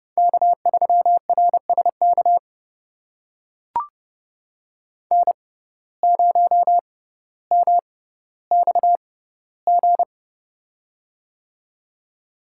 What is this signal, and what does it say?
Morse code, heard as a keyed tone, about 32 wpm, spaced as for 16 wpm, 700 Hz. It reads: K3RSK N0MXG
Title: USA Call Signs - 5 Characters - 6x Character Spacing - 30wpm